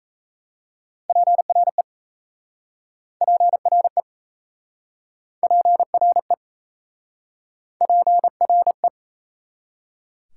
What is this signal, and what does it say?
Morse code, heard as a keyed tone, about 37 wpm, 690 Hz. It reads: PRE PRE PRE PRE